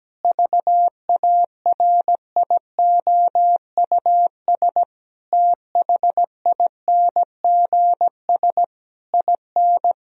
Morse code, heard as a keyed tone, about 17 wpm, 700 Hz. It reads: VARIOUS THINGS IN
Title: Sentences from Top 1000 Words - Review Part 4 - 17wpm